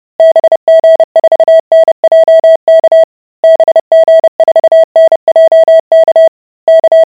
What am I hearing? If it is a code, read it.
Morse code, heard as a keyed tone, about 30 wpm, 650 Hz. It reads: BG4NJK BG4NJK K